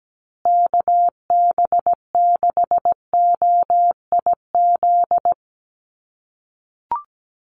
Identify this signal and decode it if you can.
Morse code, heard as a keyed tone, about 17 wpm, 700 Hz. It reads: KB6OIZ
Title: USA Call Signs Repeated 2x 17wpm